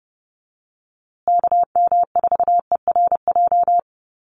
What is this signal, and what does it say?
Morse code, heard as a keyed tone, about 30 wpm, 700 Hz. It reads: KM4ERJ
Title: USA Call Signs Repeated 2x 30wpm